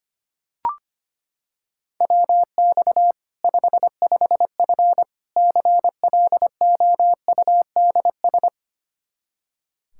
Morse code, heard as a keyed tone, about 25 wpm, 700 Hz. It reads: WX 55F CLOUDS
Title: Practice QSO - Exchange 2 - Info - 25wpm